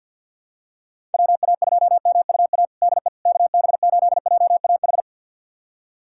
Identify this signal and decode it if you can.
Morse code, heard as a keyed tone, about 52 wpm, 700 Hz. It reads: WA2MUA DE KB8JAH
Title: USA Callsign DE Callsign 50wpm